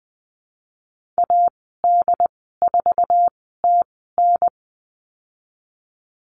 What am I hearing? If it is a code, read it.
Morse code, heard as a keyed tone, about 20 wpm, spaced as for 14 wpm, 700 Hz. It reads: AD4TN
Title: 069 Lesson 12g - Letter d - Call Signs 20wpm